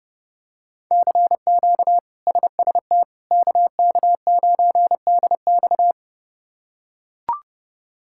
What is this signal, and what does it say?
Morse code, heard as a keyed tone, about 30 wpm, 700 Hz. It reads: CQ SST KK9DX
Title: K1USN SST QSO Practice - 30wpm